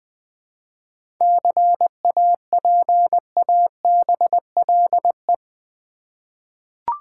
Morse code, heard as a keyed tone, about 20 wpm, 700 Hz. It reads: CAPABLE
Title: Seven Letter Words - 20wpm